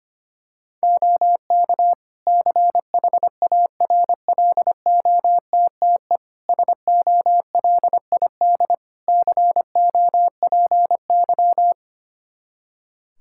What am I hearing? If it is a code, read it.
Morse code, heard as a keyed tone, about 25 wpm, 700 Hz. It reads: OK CHARLOTTE SOLID COPY
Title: Practice QSO - Exchange 3 - More Info - 25wpm